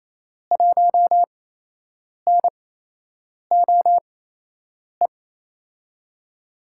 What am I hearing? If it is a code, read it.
Morse code, heard as a keyed tone, about 28 wpm, spaced as for 8 wpm, 700 Hz. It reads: 1NOE
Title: USA Call Signs - 5 Characters - 8x Character Spacing - 28wpm